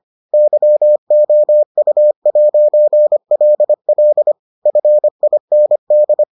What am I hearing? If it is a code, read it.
Morse code, heard as a keyed tone, about 25 wpm, 600 Hz. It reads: YOU'LL FIND